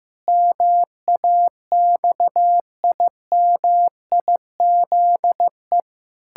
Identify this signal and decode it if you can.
Morse code, heard as a keyed tone, about 15 wpm, 700 Hz. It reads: MAXIMIZE
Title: Pangram Challenge v2 - 15wpm